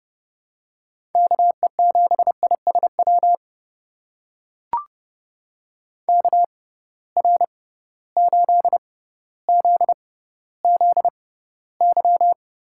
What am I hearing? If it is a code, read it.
Morse code, heard as a keyed tone, about 30 wpm, spaced as for 15 wpm, 700 Hz. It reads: KE7ISW KR8ZZY
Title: USA Call Signs - 6 Characters - 6x Character Spacing - 30wpm